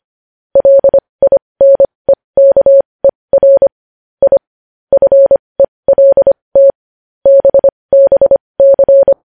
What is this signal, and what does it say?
Morse code, heard as a keyed tone, about 25 wpm, spaced as for 20 wpm, 550 Hz. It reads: LINEKER I FELT BBC
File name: podcast / News headlines-1x-25x20